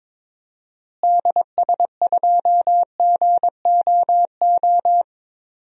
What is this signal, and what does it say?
Morse code, heard as a keyed tone, about 22 wpm, 700 Hz. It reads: DS2GOO
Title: DXCC-100 Call Signs - 6 Characters - Repeated 3x - 22wpm